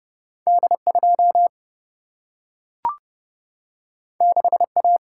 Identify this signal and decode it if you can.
Morse code, heard as a keyed tone, about 30 wpm, 700 Hz. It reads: D 2 6 A
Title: Single-Letter Single-Number 30wpm